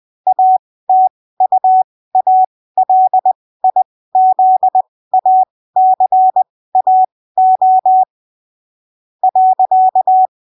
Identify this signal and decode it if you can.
Morse code, heard as a keyed tone, about 20 wpm, spaced as for 15 wpm, 750 Hz. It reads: ATUALIZACAO .
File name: arquivos / texto / texto 22 20ppm